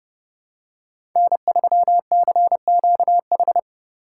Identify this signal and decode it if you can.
Morse code, heard as a keyed tone, about 30 wpm, 700 Hz. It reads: N3CQH
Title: USA Call Signs - 5 Characters - 4x Character Spacing - 30wpm